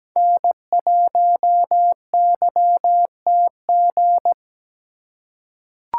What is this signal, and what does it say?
Morse code, heard as a keyed tone, about 17 wpm, 700 Hz. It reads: N1YTG
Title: USA Call Signs - 5 Characters - 2x Character Spacing - 17wpm